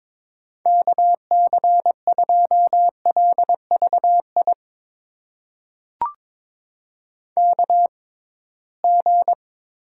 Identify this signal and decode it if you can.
Morse code, heard as a keyed tone, about 22 wpm, spaced as for 17 wpm, 700 Hz. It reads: KC2LVI KG
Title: USA Call Signs - 6 Characters - 6x Character Spacing - 22wpm